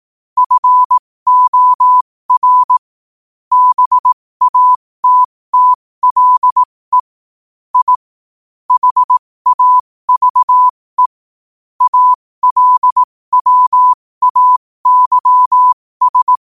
Morse code, heard as a keyed tone, about 18 wpm, spaced as for 15 wpm, 1000 Hz. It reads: FOR BATTLE I HAVE ALWAYS
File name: qotdfiles / 15wpm